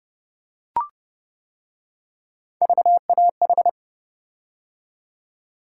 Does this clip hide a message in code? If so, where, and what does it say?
Yes: Morse code from 2.6 to 3.7 s, heard as a keyed tone, about 28 wpm, 700 Hz. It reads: VAH